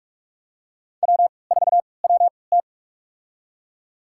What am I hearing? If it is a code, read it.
Morse code, heard as a keyed tone, about 44 wpm, spaced as for 25 wpm, 700 Hz. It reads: W4WT
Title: USA Call Signs - 3x Character Spacing - 45wpm